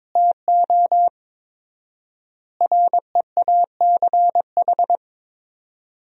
Morse code, heard as a keyed tone, about 22 wpm, 700 Hz. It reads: TO REACH
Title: 4x Word-Spacing - Sentences from Top 400 Words - 22wpm